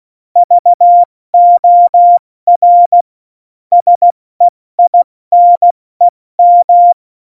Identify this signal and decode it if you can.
Morse code, heard as a keyed tone, about 16 wpm, 700 Hz. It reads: VOR SEINEM
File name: Morsekurs / Witze / Witz 22. Januar 2019, 70 F80